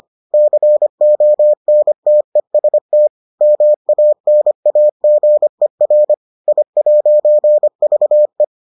Morse code, heard as a keyed tone, about 25 wpm, 600 Hz. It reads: CONTEST MANAGER I'VE